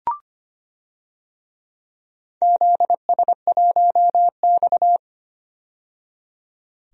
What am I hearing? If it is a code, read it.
Morse code, heard as a keyed tone, about 25 wpm, 700 Hz. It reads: ZS1X
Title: DXCC-100 Call Signs - 4 Characters - 25wpm